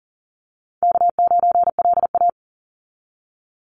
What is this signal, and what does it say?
Morse code, heard as a keyed tone, about 37 wpm, 700 Hz. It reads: K9LA
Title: Call Signs - 40wpm